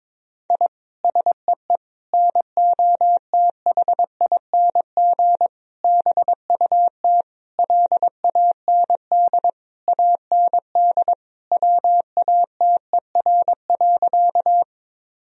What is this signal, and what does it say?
Morse code, heard as a keyed tone, about 22 wpm, 700 Hz. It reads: I SEE NOTHING BUT LAND AND WATER.
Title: Sentences from Top 1000 Words - Review Part 1 - 22wpm